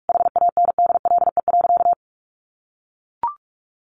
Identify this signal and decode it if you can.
Morse code, heard as a keyed tone, about 44 wpm, 700 Hz. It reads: HANDLE.
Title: Sentences from Top 1100 Words 45wpm